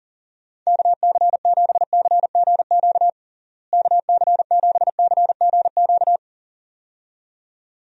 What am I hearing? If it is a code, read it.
Morse code, heard as a keyed tone, about 40 wpm, 700 Hz. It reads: KC7CGQ KC7CGQ
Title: USA Call Signs - 6 Characters - Repeated 2x - 40wpm